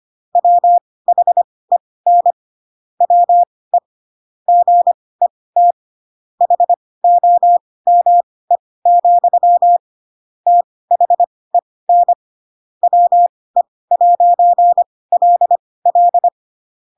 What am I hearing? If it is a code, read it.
Morse code, heard as a keyed tone, about 25 wpm, spaced as for 18 wpm, 700 Hz. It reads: WHEN WE GET HOME, THEN WE'LL
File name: Pinocchio ch3 12 25 ik1hge